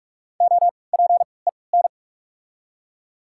Morse code, heard as a keyed tone, about 46 wpm, 690 Hz. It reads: O P E N
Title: 3x Character Spacing - Top 500 Words - 45wpm